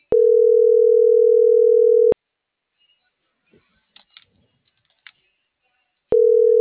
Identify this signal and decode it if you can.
Ringback tone (North America), 440 + 480 Hz.